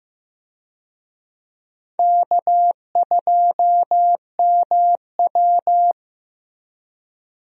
Morse code, heard as a keyed tone, about 15 wpm, 700 Hz. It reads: K2MW
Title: USA Call Signs - 4 Characters - Repeated 2x - 15wpm